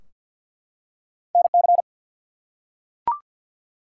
Morse code, heard as a keyed tone, about 48 wpm, 700 Hz. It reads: NC
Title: US States and Canadian Provinces 50wpm